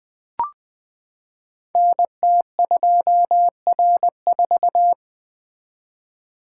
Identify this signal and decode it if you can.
Morse code, heard as a keyed tone, about 20 wpm, 700 Hz. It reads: NT2R4
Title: Code Groups - 5 Letter:Numbers - 20wpm